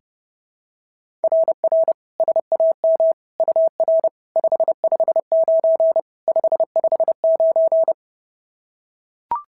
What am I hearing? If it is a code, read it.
Morse code, heard as a keyed tone, about 30 wpm, 690 Hz. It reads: RR SAM UR 559 559
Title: SOTA QSO Practice 30wpm